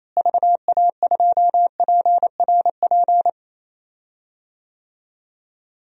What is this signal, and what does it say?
Morse code, heard as a keyed tone, about 27 wpm, 700 Hz. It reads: VA2PRP
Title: DXCC-100 Call Signs - 6 Characters - Repeated 2x - 28wpm